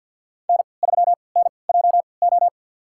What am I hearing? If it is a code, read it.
Morse code, heard as a keyed tone, about 52 wpm, spaced as for 29 wpm, 690 Hz. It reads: N3NJO
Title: USA Call Signs - 5 Characters - 3x Character Spacing - 50wpm